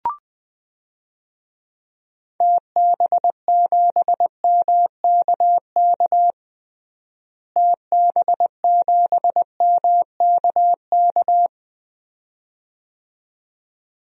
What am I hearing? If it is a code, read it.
Morse code, heard as a keyed tone, about 20 wpm, 700 Hz. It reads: TB7MKK TB7MKK
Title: DXCC-100 Call Signs - 6 Characters - Repeated 2x - 20wpm